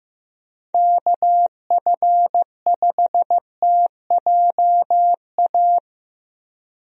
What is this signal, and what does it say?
Morse code, heard as a keyed tone, about 15 wpm, 700 Hz. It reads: KF5TJA
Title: USA Call Signs - 6 Characters - Repeated 2x - 15wpm